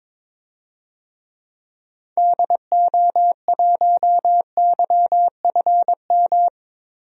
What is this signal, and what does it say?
Morse code, heard as a keyed tone, about 22 wpm, 700 Hz. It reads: DO1YFM
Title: DXCC-100 Call Signs - 6 Characters - Repeated 2x - 22wpm